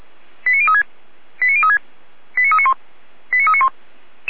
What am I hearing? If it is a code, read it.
Five-tone selective call (ZVEI), heard as quick runs of five tones, tones of about 80 ms. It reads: 89036 89036 89391 89391
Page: ZVEI